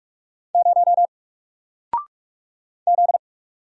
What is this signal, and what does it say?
Morse code, heard as a keyed tone, about 46 wpm, 700 Hz. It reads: 0 Z
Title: Single Letter-Number Alt 45wpm